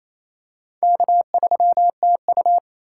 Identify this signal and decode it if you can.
Morse code, heard as a keyed tone, about 29 wpm, 700 Hz. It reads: K3TU
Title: USA Call Signs - 3x Character Spacing - 28wpm